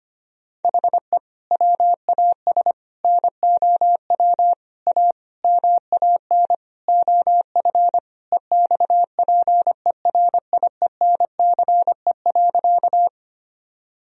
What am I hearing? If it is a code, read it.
Morse code, heard as a keyed tone, about 25 wpm, 700 Hz. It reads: HE WAS NOW A MAN OF EXPERIENCE.
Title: Sentences from Top 1000 Words - Review Part 3 - 25wpm